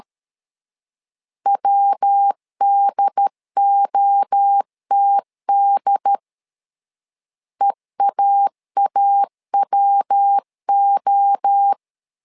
Morse code, heard as a keyed tone, about 13 wpm, 800 Hz. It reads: WDOTD EAAWO
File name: LEZIONI / 3 MWRO / 3 MWRO 12wpm 1730